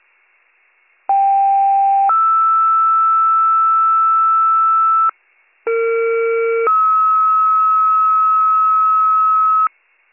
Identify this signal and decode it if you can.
Two-tone page, heard as a one-second tone, then a three-second tone, about 790 then 1320 Hz, about 460 then 1290 Hz. It is heard from 1.1 to 9.7 s.